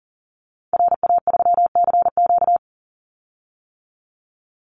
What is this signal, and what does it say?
Morse code, heard as a keyed tone, about 40 wpm, 700 Hz. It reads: RA3CQ
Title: Call Signs - 40wpm